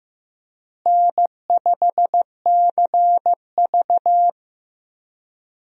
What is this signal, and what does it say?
Morse code, heard as a keyed tone, about 16 wpm, 700 Hz. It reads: N5CV
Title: USA Call Signs - 4 Characters - Repeated 3x - 15wpm